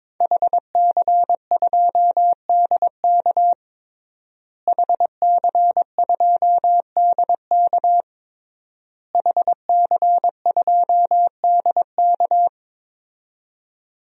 Morse code, heard as a keyed tone, about 22 wpm, 700 Hz. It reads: HC2DK HC2DK HC2DK
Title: DXCC-100 Call Signs - 5 Characters - Repeated 3x - 22wpm